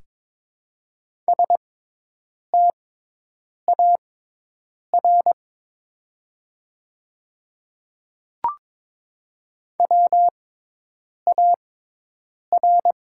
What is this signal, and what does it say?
Morse code, heard as a keyed tone, about 22 wpm, spaced as for 8 wpm, 700 Hz. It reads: STAR WAR
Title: 6x Character Spacing - Top 500 Words - 22wpm